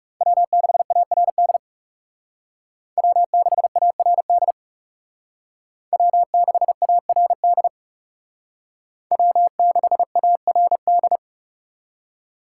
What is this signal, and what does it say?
Morse code, heard as a keyed tone, about 38 wpm, 700 Hz. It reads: W6ARD W6ARD W6ARD W6ARD